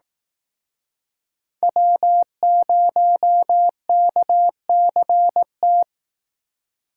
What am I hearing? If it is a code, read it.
Morse code, heard as a keyed tone, about 18 wpm, 700 Hz. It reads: W0KCT
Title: Speed Racing - USA 1x3 Call Signs 15wpm